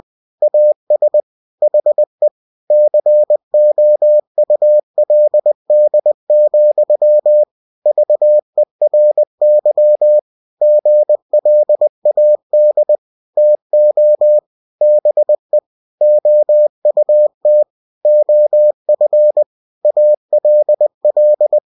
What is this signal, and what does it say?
Morse code, heard as a keyed tone, about 20 wpm, 600 Hz. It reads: AS HE COULD, VERY GLAD TO BE OUT OF ALL
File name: cw-books / winnie-the-pooh / winnie-20wpm-page-0024